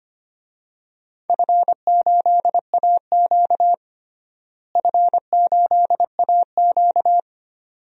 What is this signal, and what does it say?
Morse code, heard as a keyed tone, about 25 wpm, 700 Hz. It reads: F8AQ F8AQ
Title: DXCC-100 Call Signs - 4 Characters - Repeated 2x - 25wpm